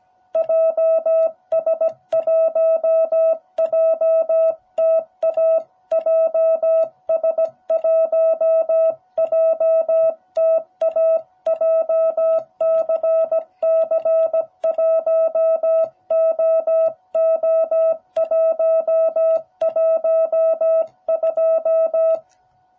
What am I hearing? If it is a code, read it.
Morse code, heard as a keyed tone, about 17 wpm, 660 Hz. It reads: JS1JTA JS1JTA JCC1OO112